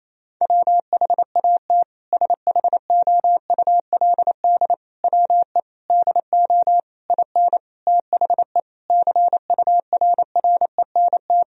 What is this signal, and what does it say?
Morse code, heard as a keyed tone, about 28 wpm, 700 Hz. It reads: WHAT SHOULD WE DO IN THE CURRENT
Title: Sentences from Top 1000 Words - Review Part 4 - 28wpm